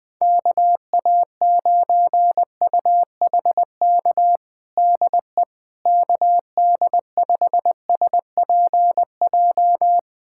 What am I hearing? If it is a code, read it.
Morse code, heard as a keyed tone, about 20 wpm, 700 Hz. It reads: KA9UHK DE KD5SPJ
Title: USA Callsign DE Callsign - 6 Characters - 20wpm